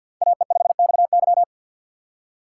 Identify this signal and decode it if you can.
Morse code, heard as a keyed tone, about 49 wpm, 700 Hz. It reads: AE5XY
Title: USA Call Signs - 5 Characters - 4x Character Spacing - 50wpm